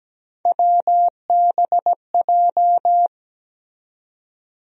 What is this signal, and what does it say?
Morse code, heard as a keyed tone, about 17 wpm, 700 Hz. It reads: WBJ
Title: USA Call Sign Postfixes - 3 Letters 17wpm